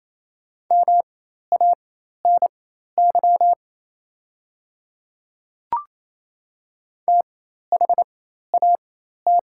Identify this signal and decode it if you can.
Morse code, heard as a keyed tone, about 28 wpm, spaced as for 13 wpm, 690 Hz. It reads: MANY THAT
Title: 4x Character Spacing - Top 300 Words - 28wpm